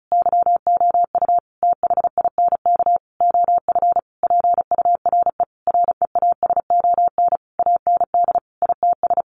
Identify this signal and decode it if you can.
Morse code, heard as a keyed tone, about 34 wpm, 700 Hz. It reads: YOU THINK OF PURE REASON AND ITS
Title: Sentences from Top 1500 Words 35wpm